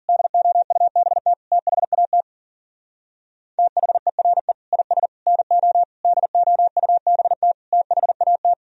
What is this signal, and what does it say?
Morse code, heard as a keyed tone, about 50 wpm, 700 Hz. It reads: DOUBT THAT THERE IS NO DOUBT THAT